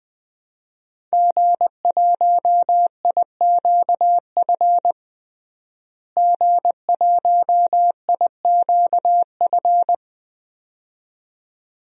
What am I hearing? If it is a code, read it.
Morse code, heard as a keyed tone, about 20 wpm, 700 Hz. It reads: G1IQF G1IQF
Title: DXCC-100 Call Signs - 5 Characters - Repeated 2x - 20wpm